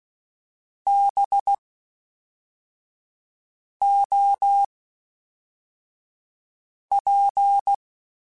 Morse code, heard as a keyed tone, about 16 wpm, 800 Hz. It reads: B O P
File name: Misc / Misc-mp3s / Jumps @10